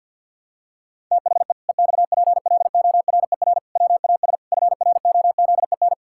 Morse code, heard as a keyed tone, about 50 wpm, 700 Hz. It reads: THE EXPLORER WAS FROZEN
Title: Pangram Challenge v2 - 50wpm